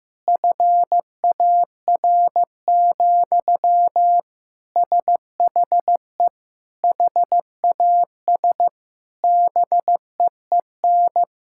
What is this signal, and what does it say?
Morse code, heard as a keyed tone, about 15 wpm, 700 Hz. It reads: FAR, SHE HAS BEEN
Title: Sentences from Top 1000 Words - Review Part 6 - 15wpm